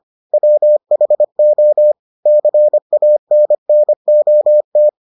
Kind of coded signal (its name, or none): Morse code